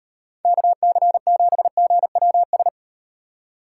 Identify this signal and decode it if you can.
Morse code, heard as a keyed tone, about 35 wpm, 700 Hz. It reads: KC7GWS